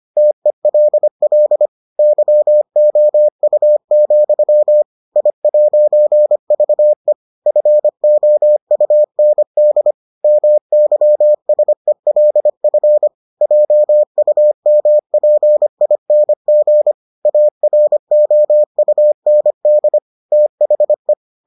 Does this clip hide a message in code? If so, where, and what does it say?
Yes: Morse code from 0.2 to 21.1 s, heard as a keyed tone, about 25 wpm, 600 Hz. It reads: TELL YOU, I'VE FOUND MYSELF JUMPING AROUND THE